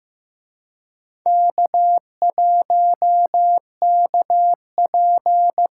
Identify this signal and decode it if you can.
Morse code, heard as a keyed tone, about 18 wpm, 700 Hz. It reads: K1KP
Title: Speed Racing - USA Call Signs 15wpm